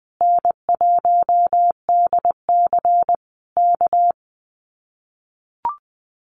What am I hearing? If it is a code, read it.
Morse code, heard as a keyed tone, about 20 wpm, 700 Hz. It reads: N1DC K
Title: Common QSO Phrases Encore 20wpm